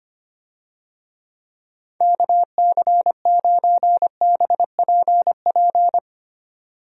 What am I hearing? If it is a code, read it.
Morse code, heard as a keyed tone, about 25 wpm, 700 Hz. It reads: KC9BPP
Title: USA Call Signs - 6 Characters - 25wpm